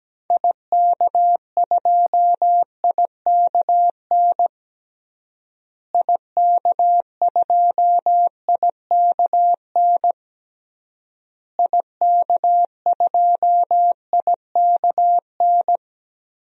Morse code, heard as a keyed tone, about 17 wpm, 700 Hz. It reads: IK2IKN IK2IKN IK2IKN
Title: DXCC-100 Call Signs - 6 Characters - Repeated 3x - 17wpm